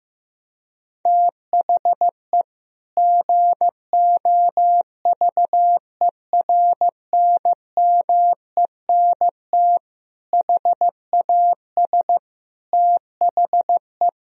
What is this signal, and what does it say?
Morse code, heard as a keyed tone, about 16 wpm, 700 Hz. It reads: THE GOVERNMENT HAS THE
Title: Sentences from Top 1000 Words - Review Part 3 - 15wpm